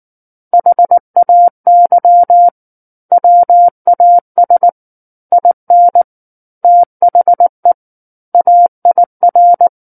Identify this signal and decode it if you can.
Morse code, heard as a keyed tone, about 19 wpm, 700 Hz. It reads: HAY WAS IN THE AIR
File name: Book One / 10-Chapter 10 - 19 WPM